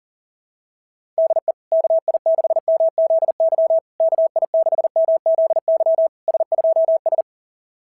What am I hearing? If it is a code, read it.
Morse code, heard as a keyed tone, about 40 wpm, 690 Hz. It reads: DE KI6MZY KI6MZY S2S
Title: SOTA QSO Practice 40wpm